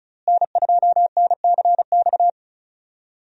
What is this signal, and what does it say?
Morse code, heard as a keyed tone, about 36 wpm, 700 Hz. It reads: N2NCX
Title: USA Call Signs - 5 Characters - Repeated 2x - 35wpm